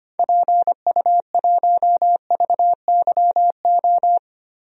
Morse code, heard as a keyed tone, about 25 wpm, 700 Hz. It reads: PU1VYO
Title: DXCC-100 Call Signs - 6 Characters - Repeated 2x - 25wpm